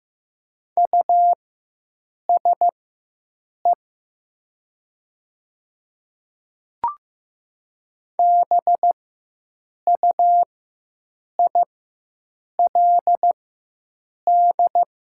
Morse code, heard as a keyed tone, about 15 wpm, spaced as for 7 wpm, 700 Hz. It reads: USE BUILD
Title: 4x Character Spacing - Top 200 Words - 15wpm